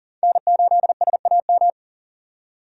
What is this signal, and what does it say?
Morse code, heard as a keyed tone, about 40 wpm, 700 Hz. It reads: N8SAM
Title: USA Call Signs - 4x Character Spacing - 40wpm